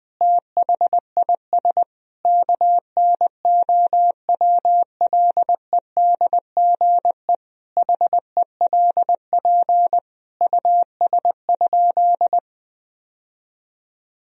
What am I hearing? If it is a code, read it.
Morse code, heard as a keyed tone, about 20 wpm, 700 Hz. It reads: THIS KNOWLEDGE HELP US?
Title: Sentences from Top 1000 Words - Review Part 3 - 20wpm